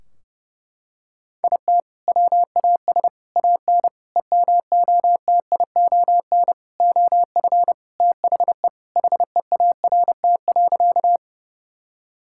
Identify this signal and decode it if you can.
Morse code, heard as a keyed tone, about 30 wpm, 700 Hz. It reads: IT WAS AN EMOTION OF THE HEART.
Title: Sentences from Top 1600 Words 30wpm